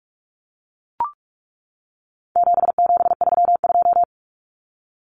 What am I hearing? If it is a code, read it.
Morse code, heard as a keyed tone, about 44 wpm, 700 Hz. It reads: 7732
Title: Common Cut Numbers - 4-Digit 45wpm